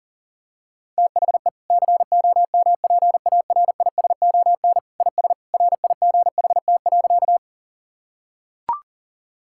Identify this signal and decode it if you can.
Morse code, heard as a keyed tone, about 40 wpm, 700 Hz. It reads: THE COMPARISON IS RIGHT.
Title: Sentences from Top 1600 Words 40wpm